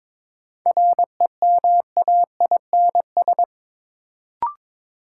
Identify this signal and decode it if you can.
Morse code, heard as a keyed tone, about 22 wpm, 700 Hz. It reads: REMAINS
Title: Sets of 5 Words Encore - New Word Drill - 22wpm